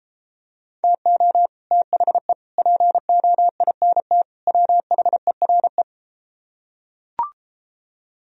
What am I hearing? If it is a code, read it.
Morse code, heard as a keyed tone, about 33 wpm, 700 Hz. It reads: TO THE POINT WHERE